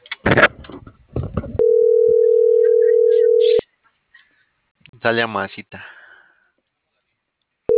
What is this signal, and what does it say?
Ringback tone (North America), 440 + 480 Hz, with speech.
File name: CENTRAL MABE